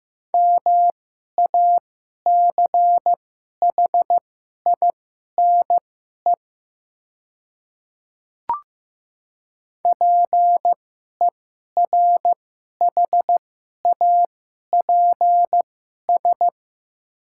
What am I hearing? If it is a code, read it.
Morse code, heard as a keyed tone, about 15 wpm, spaced as for 11 wpm, 700 Hz. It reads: MACHINE PERHAPS
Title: 2x Character Spacing - Top 500 Words - 15wpm